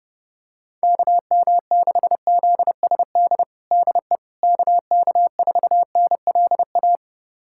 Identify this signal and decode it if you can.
Morse code, heard as a keyed tone, about 31 wpm, 700 Hz. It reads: KM6ZSD DE KK4NLA
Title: USA Callsign DE Callsign - 6 Characters - 30wpm